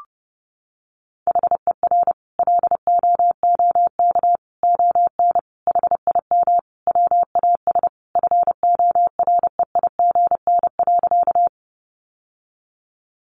Morse code, heard as a keyed tone, about 30 wpm, 700 Hz. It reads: HER LOOK ON HIM WAS FOREIGN.
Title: Sentences from Top 1400 Words 30wpm